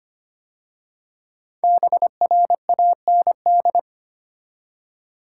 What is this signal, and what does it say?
Morse code, heard as a keyed tone, about 25 wpm, 700 Hz. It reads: BRAND
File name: 5 Letter Words - First 3 Letters Given - 25wpm